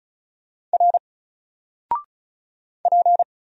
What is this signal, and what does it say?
Morse code, heard as a keyed tone, about 34 wpm, 700 Hz. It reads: R P
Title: Single Letter-Number 35wpm